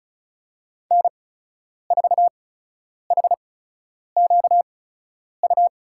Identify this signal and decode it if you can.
Morse code, heard as a keyed tone, about 36 wpm, spaced as for 10 wpm, 700 Hz. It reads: N4HQU
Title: USA Call Signs - 5 Characters - 8x Character Spacing - 35wpm